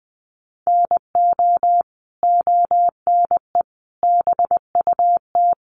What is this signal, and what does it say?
Morse code, heard as a keyed tone, about 20 wpm, 700 Hz. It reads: NO ONE BUT
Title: Sentences from Top 1300 Words 20wpm